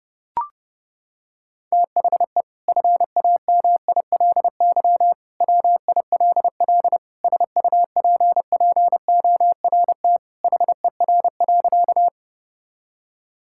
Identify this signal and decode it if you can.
Morse code, heard as a keyed tone, about 32 wpm, 700 Hz. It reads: THE FAMILY WILL SUPPORT HER.
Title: Sentences from Top 1000 Words - Review Part 5 - 30wpm